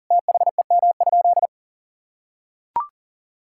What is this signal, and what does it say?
Morse code, heard as a keyed tone, about 40 wpm, 700 Hz. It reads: THEM?
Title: Sentences from Top 1000 Words - Review Part 5 - 40wpm